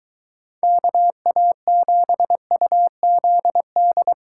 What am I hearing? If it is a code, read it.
Morse code, heard as a keyed tone, about 23 wpm, 700 Hz. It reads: KA7UZD